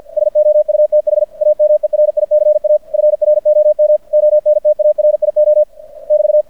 Morse code, heard as a keyed tone, about 50 wpm, 600 Hz. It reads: SOUTH AMERICA FROM ONTARIO <BT>
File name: archived / CRSnet